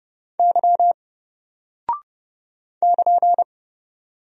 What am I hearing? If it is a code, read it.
Morse code, heard as a keyed tone, about 30 wpm, 700 Hz. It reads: Y <KN>
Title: Single Character (Letter:Number:Punctuation:Prosign) 30wpm